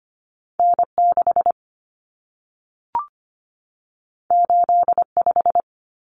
Morse code, heard as a keyed tone, about 25 wpm, 700 Hz. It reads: N6 85
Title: Common Cut Numbers - 2-Digit 25wpm